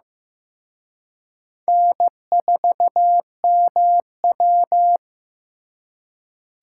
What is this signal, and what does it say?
Morse code, heard as a keyed tone, about 18 wpm, 700 Hz. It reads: N4MW
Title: Speed Racing - USA 1x2 Call Signs 15wpm